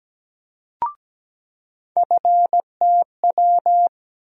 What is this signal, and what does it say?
Morse code, heard as a keyed tone, about 17 wpm, 700 Hz. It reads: FTW
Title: USA Call Sign Postfixes - 3 Letters 17wpm